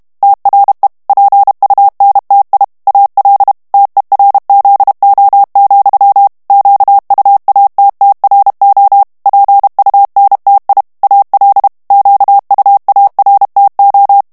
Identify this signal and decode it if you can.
Morse code, heard as a keyed tone, about 32 wpm, 800 Hz. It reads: TRE PUNTI AL TERZO, QUATTRO PUNTI AL QUARTO
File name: ITALIEN texte clair 2 vitesse 30 mots minute